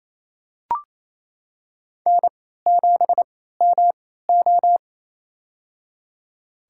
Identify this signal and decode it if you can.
Morse code, heard as a keyed tone, about 28 wpm, spaced as for 16 wpm, 700 Hz. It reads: N7MO
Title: USA Call Signs - 3x Character Spacing - 28wpm